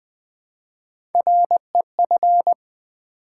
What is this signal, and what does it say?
Morse code, heard as a keyed tone, about 20 wpm, 700 Hz. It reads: REF
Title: Build A Word - 1 Letter At A Time - 3 to 5-Letter Words - 20wpm